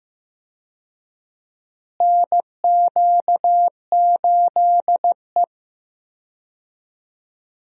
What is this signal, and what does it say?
Morse code, heard as a keyed tone, about 16 wpm, 700 Hz. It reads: NQ8E
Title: USA Call Signs - 4 Characters - Repeated 2x - 15wpm